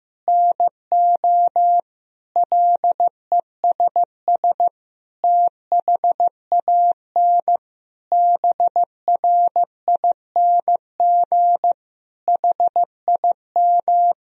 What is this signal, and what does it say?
Morse code, heard as a keyed tone, about 15 wpm, 700 Hz. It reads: NO LESS THAN BRING HIM
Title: Sentences from Top 1000 Words - Review Part 3 - 15wpm